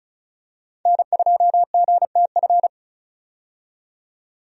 Morse code, heard as a keyed tone, about 36 wpm, 700 Hz. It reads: N2GTF
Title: USA Call Signs - 5 Characters - 35wpm